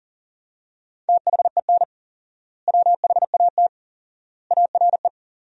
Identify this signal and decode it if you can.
Morse code, heard as a keyed tone, about 40 wpm, 700 Hz. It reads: THEN WHAT ARE
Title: 4x Word-Spacing - Sentences from Top 100 Words - 40wpm